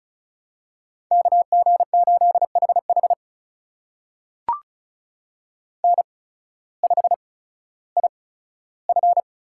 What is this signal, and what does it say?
Morse code, heard as a keyed tone, about 35 wpm, spaced as for 16 wpm, 690 Hz. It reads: KG8HH N5IF
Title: USA Call Signs - 5 Characters - 8x Character Spacing - 35wpm